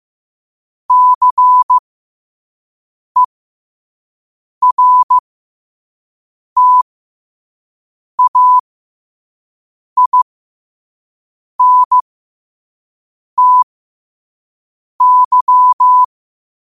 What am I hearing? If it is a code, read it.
Morse code, heard as a keyed tone, about 15 wpm, spaced as for 5 wpm, 1000 Hz. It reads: CERTAINTY
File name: qotdfiles / 5wpm